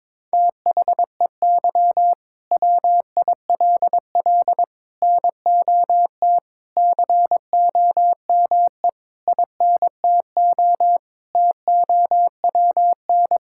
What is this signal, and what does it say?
Morse code, heard as a keyed tone, about 22 wpm, 700 Hz. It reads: THEY WILL NOT COME INTO TOWN
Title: Sentences from Top 1000 Words - Review Part 5 - 22wpm